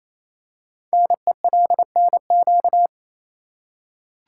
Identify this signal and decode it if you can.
Morse code, heard as a keyed tone, about 27 wpm, 700 Hz. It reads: NELNQ
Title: Code Groups - 5 Letters - 28wpm